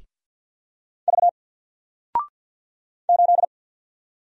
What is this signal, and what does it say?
Morse code, heard as a keyed tone, about 48 wpm, 700 Hz. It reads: V 8
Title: Single Letter-Number Alt 50wpm